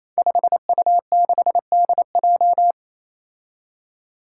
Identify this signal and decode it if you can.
Morse code, heard as a keyed tone, about 29 wpm, 700 Hz. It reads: 5U6DJ
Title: Five Challenging Letter-Number Combinations 28wpm